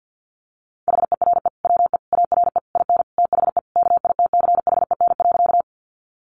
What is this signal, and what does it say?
Morse code, heard as a keyed tone, about 50 wpm, 700 Hz. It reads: HERE WE ARE IN THE KITCHEN.
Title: Sentences from Top 1400 Words 50wpm